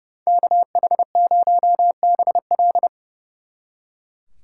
Morse code, heard as a keyed tone, about 36 wpm, 700 Hz. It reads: KH0BL
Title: Speed Racing - USA 2x2 Call Signs 30wpm